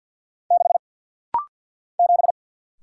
Morse code, heard as a keyed tone, about 48 wpm, 700 Hz. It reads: 6 7
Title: Single Letter-Number - Rapid-Fire 50wpm